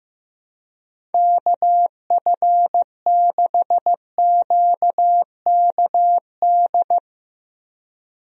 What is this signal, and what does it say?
Morse code, heard as a keyed tone, about 15 wpm, 700 Hz. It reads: KF6QKD
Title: USA Call Signs - 6 Characters - 15wpm